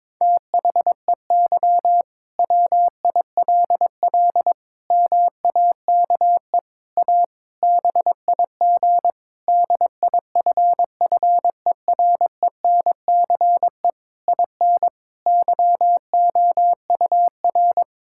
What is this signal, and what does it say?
Morse code, heard as a keyed tone, about 22 wpm, 700 Hz. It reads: THEY WILL MAKE A BIG DIFFERENCE IN YOUR
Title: Sentences from Top 1000 Words - Review Part 5 - 22wpm